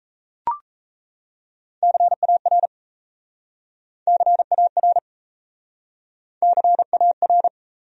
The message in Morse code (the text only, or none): CAR CAR CAR